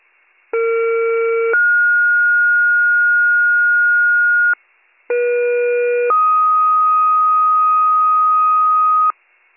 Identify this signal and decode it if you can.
Two-tone page, heard as a one-second tone, then a three-second tone, about 460 then 1430 Hz, about 480 then 1250 Hz, with noise. It is heard from 0.5 to 9.1 s.